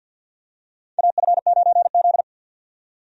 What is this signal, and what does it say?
Morse code, heard as a keyed tone, about 46 wpm, 700 Hz. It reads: AF9Z
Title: USA Call Signs - 4 Characters - 50wpm